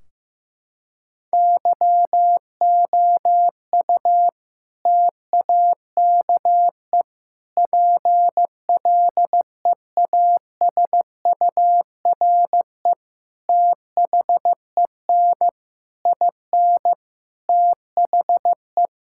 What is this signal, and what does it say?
Morse code, heard as a keyed tone, about 15 wpm, 700 Hz. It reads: YOU TAKE PLEASURE THEN IN THE
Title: Sentences from Top 1600 Words 15wpm